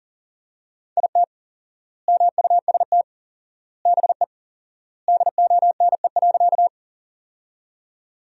Morse code, heard as a keyed tone, about 40 wpm, 700 Hz. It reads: IT MUST BE DONE.
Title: 4x Word-Spacing - Sentences from Top 500 Words - 40wpm